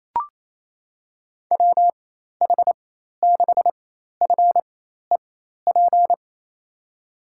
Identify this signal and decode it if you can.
Morse code, heard as a keyed tone, about 28 wpm, spaced as for 13 wpm, 700 Hz. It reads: WH6FEP
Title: USA Call Signs - 6 Characters - 4x Character Spacing - 28wpm